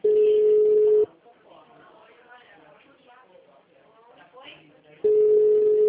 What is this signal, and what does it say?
Ringback tone (Europe, 425 Hz), with speech.